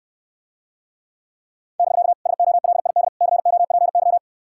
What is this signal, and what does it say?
Morse code, heard as a keyed tone, about 69 wpm, 700 Hz. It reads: <BK> SOLID COPY